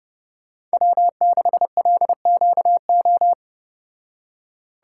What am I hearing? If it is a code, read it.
Morse code, heard as a keyed tone, about 34 wpm, 700 Hz. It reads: W6LQO